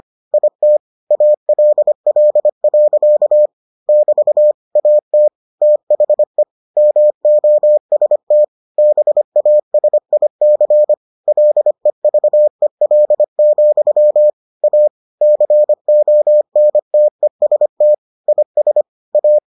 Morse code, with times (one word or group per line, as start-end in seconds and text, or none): 0.3-0.8 IT
1.1-3.5 ALL.
3.9-4.5 <BT>
4.8-5.3 AT
5.6-6.4 THE
6.8-8.4 MOST
8.8-10.9 BASIC
11.3-14.3 LEVEL,
14.6-14.9 A
15.2-17.9 CONTEST
18.3-18.8 IS
19.1-19.4 A